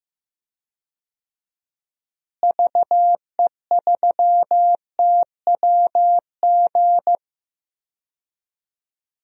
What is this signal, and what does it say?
Morse code, heard as a keyed tone, about 15 wpm, 700 Hz. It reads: VE3TWG
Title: DXCC-100 Call Signs - 6 Characters - Repeated 3x - 15wpm